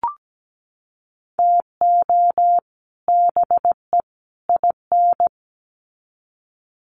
Morse code, heard as a keyed tone, about 17 wpm, 700 Hz. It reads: TO BE IN
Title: Sets of 3 Words 17wpm